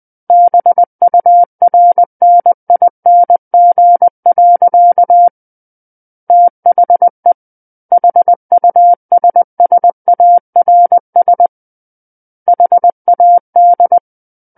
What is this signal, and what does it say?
Morse code, heard as a keyed tone, about 20 wpm, 700 Hz. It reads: BURNING. THE HUSSARS HAD
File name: 20 WPM